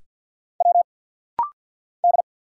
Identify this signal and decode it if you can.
Morse code, heard as a keyed tone, about 50 wpm, 700 Hz. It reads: W D
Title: Single Letters - Rapid-Fire 50wpm